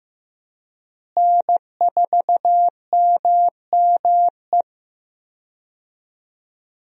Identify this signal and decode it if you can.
Morse code, heard as a keyed tone, about 15 wpm, 700 Hz. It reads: N4MME
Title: USA Call Signs - 5 Characters - Repeated 3x - 15wpm